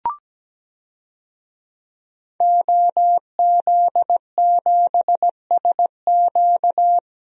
Morse code, heard as a keyed tone, about 17 wpm, 700 Hz. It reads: OZ7SQ
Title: DXCC-100 Call Signs - 5 Characters - 17wpm